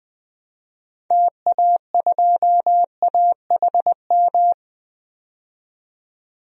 Morse code, heard as a keyed tone, about 21 wpm, 700 Hz. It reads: TA2AHM
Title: DXCC-100 Call Signs - 6 Characters - 20wpm